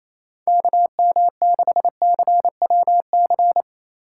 Morse code, heard as a keyed tone, about 28 wpm, 700 Hz. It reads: KM6CWC
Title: USA Call Signs - 6 Characters - 8x Character Spacing - 28wpm